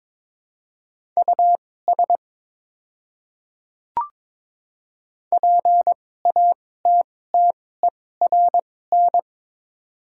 Morse code, heard as a keyed tone, about 22 wpm, spaced as for 16 wpm, 690 Hz. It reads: US PATTERN
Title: 2x Character Spacing - Top 500 Words - 22wpm